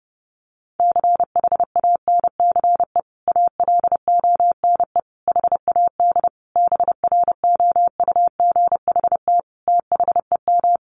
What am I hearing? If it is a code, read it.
Morse code, heard as a keyed tone, about 30 wpm, 700 Hz. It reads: CHANCE ALONE HAD BROUGHT THEM
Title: Sentences from Top 1000 Words 30wpm